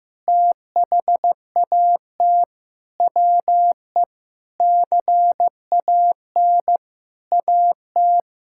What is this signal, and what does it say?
Morse code, heard as a keyed tone, about 15 wpm, 700 Hz. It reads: THAT WE CAN AT
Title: Sentences from Top 1000 Words - Review Part 5 - 15wpm